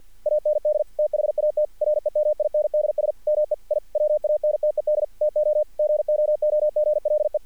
Morse code, heard as a keyed tone, about 46 wpm, 600 Hz. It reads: AND THAT REMINDS ME I WANTED TO GOOGLE
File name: archived / CRSnet